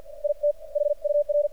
Morse code, heard as a keyed tone, about 49 wpm, spaced as for 42 wpm, 590 Hz. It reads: IT HUR
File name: archived / CRSnet